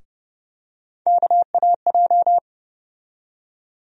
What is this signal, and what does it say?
Morse code, heard as a keyed tone, about 30 wpm, 700 Hz. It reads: KAJ
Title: USA Call Sign Postfixes - 3 Letters 30wpm